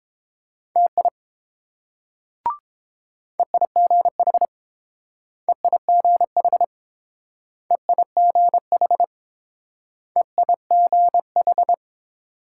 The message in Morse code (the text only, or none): TI EIGH EIGH EIGH EIGH